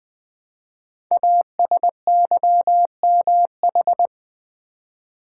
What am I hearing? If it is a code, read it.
Morse code, heard as a keyed tone, about 20 wpm, 700 Hz. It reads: ASYMH
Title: Code Groups - 5 Letters - 20wpm